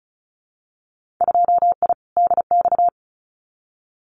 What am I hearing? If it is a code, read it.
Morse code, heard as a keyed tone, about 34 wpm, 700 Hz. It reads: 2I DX
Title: Winter Field Day Exchanges 35wpm